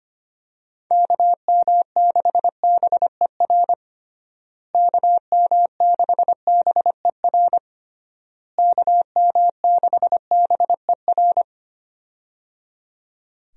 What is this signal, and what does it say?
Morse code, heard as a keyed tone, about 25 wpm, 700 Hz. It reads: KM6BER KM6BER KM6BER
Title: USA Call Signs - 6 Characters - Repeated 3x - 25wpm